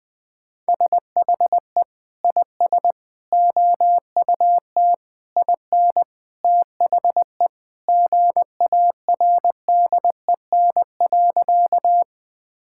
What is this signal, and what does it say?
Morse code, heard as a keyed tone, about 21 wpm, 700 Hz. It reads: SHE IS OUT IN THE GARDEN.
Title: Sentences with 3-Word Phrases - 20wpm